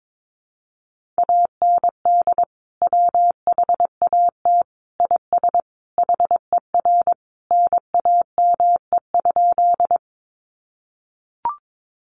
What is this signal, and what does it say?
Morse code, heard as a keyed tone, about 22 wpm, 700 Hz. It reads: AND WHAT IS HER NAME?
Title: Sentences from Top 200 Words 22wpm